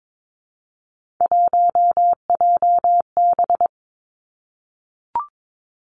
Morse code, heard as a keyed tone, about 22 wpm, 700 Hz. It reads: 1JB
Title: Three Challenging Letter-Number Combinations 22wpm